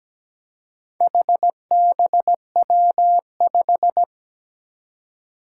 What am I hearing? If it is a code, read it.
Morse code, heard as a keyed tone, about 17 wpm, 700 Hz. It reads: HBW5
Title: Four Challenging Letter-Number Combinations 17wpm